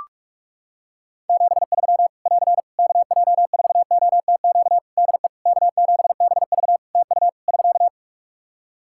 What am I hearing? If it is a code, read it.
Morse code, heard as a keyed tone, about 44 wpm, 700 Hz. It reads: 73 <AR> KJ4OTQ DE K7BV TU <SK>
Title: K1USN SST QSO Practice 45wpm